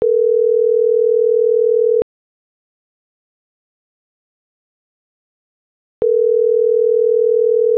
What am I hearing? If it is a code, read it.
Ringback tone (North America), 440 + 480 Hz.